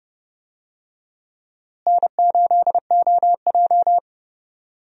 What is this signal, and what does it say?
Morse code, heard as a keyed tone, about 32 wpm, 700 Hz. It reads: N8OJ
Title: USA Call Signs - 4 Characters - Repeated 2x - 30wpm